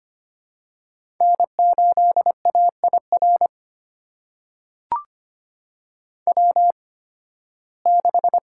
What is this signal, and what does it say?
Morse code, heard as a keyed tone, about 25 wpm, spaced as for 16 wpm, 700 Hz. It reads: N8AIR W6
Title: USA Call Signs - 5 Characters - 8x Character Spacing - 25wpm